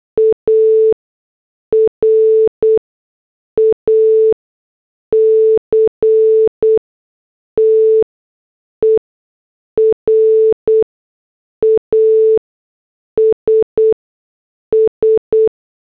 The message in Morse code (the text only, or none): ARACTER ASS